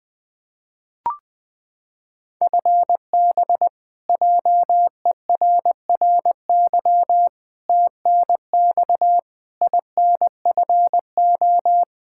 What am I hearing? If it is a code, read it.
Morse code, heard as a keyed tone, about 21 wpm, 700 Hz. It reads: FB JERRY TNX INFO
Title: K1USN SST QSO Practice 20wpm